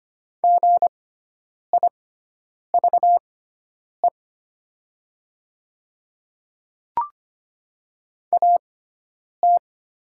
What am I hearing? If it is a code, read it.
Morse code, heard as a keyed tone, about 25 wpm, spaced as for 9 wpm, 700 Hz. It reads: GIVE AT
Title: 6x Character Spacing - Top 300 Words - 25wpm